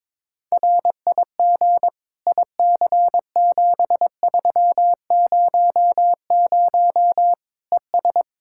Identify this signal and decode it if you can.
Morse code, heard as a keyed tone, about 22 wpm, 700 Hz. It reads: RIG IC7300 ES
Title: K1USN SST QSO Practice 22wpm